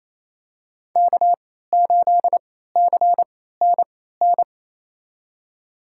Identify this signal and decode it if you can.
Morse code, heard as a keyed tone, about 28 wpm, spaced as for 16 wpm, 700 Hz. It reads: K8CNN
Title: USA Call Signs - 3x Character Spacing - 28wpm